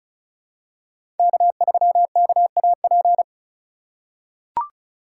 Morse code, heard as a keyed tone, about 36 wpm, 700 Hz. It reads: K3KAP
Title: USA Call Signs - 3x Character Spacing - 35wpm